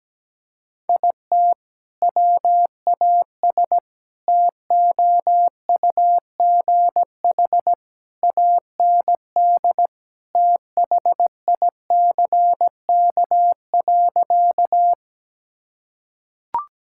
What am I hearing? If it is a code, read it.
Morse code, heard as a keyed tone, about 17 wpm, 700 Hz. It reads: IT WAS TOUGH AND THICK.
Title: Sentences from Top 1600 Words 17wpm